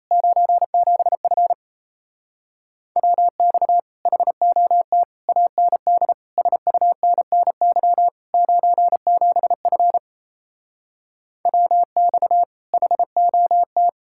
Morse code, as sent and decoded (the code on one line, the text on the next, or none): ----. --... ..-. / .-- -..- / .... --- - / .- -. -.. / ... ..- -. -. -.-- / ----. --... ..-. / .-- -..- / .... --- -
97F WX HOT AND SUNNY 97F WX HOT